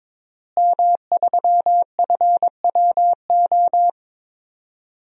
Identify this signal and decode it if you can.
Morse code, heard as a keyed tone, about 22 wpm, 700 Hz. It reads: M3FWO
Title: DXCC-100 Call Signs - 5 Characters - Repeated 3x - 22wpm